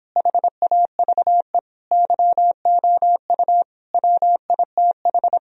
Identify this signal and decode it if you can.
Morse code, heard as a keyed tone, about 26 wpm, 700 Hz. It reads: HAVE YOU WITH